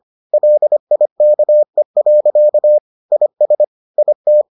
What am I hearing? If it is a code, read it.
Morse code, heard as a keyed tone, about 25 wpm, 600 Hz. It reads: LIKE. IS IT